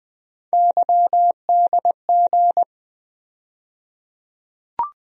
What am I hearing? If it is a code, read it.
Morse code, heard as a keyed tone, about 20 wpm, 700 Hz. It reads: YDG
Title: USA Call Sign Postfixes - 3 Letters 20wpm